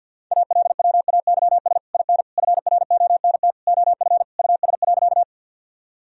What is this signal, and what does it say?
Morse code, heard as a keyed tone, about 50 wpm, 700 Hz. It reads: ALWAYS IN FRONT OF US.
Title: Sentences from Top 1000 Words - Review Part 2 - 50wpm